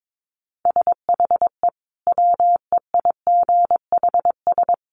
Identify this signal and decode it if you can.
Morse code, heard as a keyed tone, about 22 wpm, 700 Hz. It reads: SHE WEIGHS
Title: Easy Phrases 22wpm